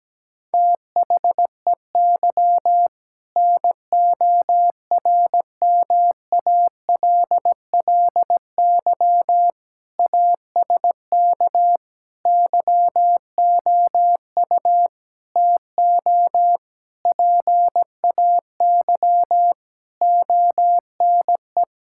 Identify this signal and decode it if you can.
Morse code, heard as a keyed tone, about 17 wpm, 700 Hz. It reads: THEY NORMALLY ASK YOU TO PAY ONE
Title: Sentences from Top 1000 Words - Review Part 6 - 17wpm